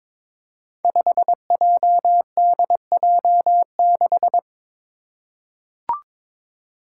Morse code, heard as a keyed tone, about 22 wpm, 700 Hz. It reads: 5JDJ6
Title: Five Challenging Letter-Number Combinations 22wpm